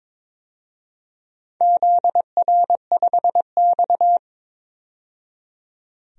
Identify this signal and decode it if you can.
Morse code, heard as a keyed tone, about 22 wpm, 700 Hz. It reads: ZR5X
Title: DXCC-100 Call Signs - 4 Characters - 22wpm